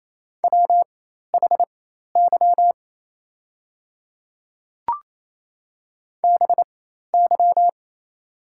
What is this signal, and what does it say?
Morse code, heard as a keyed tone, about 28 wpm, spaced as for 13 wpm, 690 Hz. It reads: WHY BY
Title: 4x Character Spacing - Top 500 Words - 28wpm